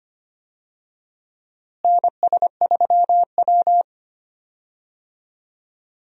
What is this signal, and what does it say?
Morse code, heard as a keyed tone, about 25 wpm, 700 Hz. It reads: NS3W
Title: USA Call Signs - 4 Characters - Repeated 3x - 25wpm